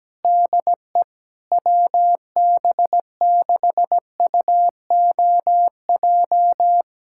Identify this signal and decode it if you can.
Morse code, heard as a keyed tone, about 17 wpm, 700 Hz. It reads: DE WB6UOJ
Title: USA Callsign DE Callsign - 6 Characters - 17wpm